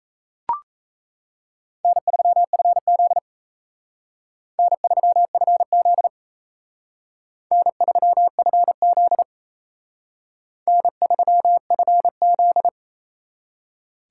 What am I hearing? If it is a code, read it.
Morse code, heard as a keyed tone, about 36 wpm, 700 Hz. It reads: N3FZ N3FZ N3FZ N3FZ